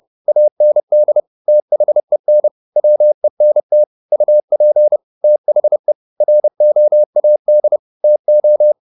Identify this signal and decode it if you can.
Morse code, heard as a keyed tone, about 30 wpm, 600 Hz. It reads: AND THEN WENT UP THE ROAD TO
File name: the old man and the sea / the old man and the sea 1 30wpm